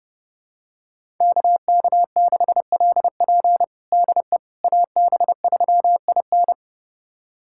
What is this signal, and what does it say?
Morse code, heard as a keyed tone, about 30 wpm, 700 Hz. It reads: KK6LP DE AB3IN
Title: USA Callsign DE Callsign - 5 Characters - 30wpm